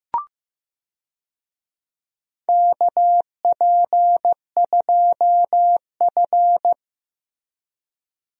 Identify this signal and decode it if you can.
Morse code, heard as a keyed tone, about 16 wpm, 700 Hz. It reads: KP2F
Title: DXCC-100 Call Signs - 4 Characters - Repeated 3x - 15wpm